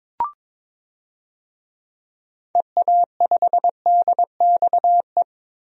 Morse code, heard as a keyed tone, about 22 wpm, 700 Hz. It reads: EA5DXE
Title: DXCC-100 Call Signs - 6 Characters - 22wpm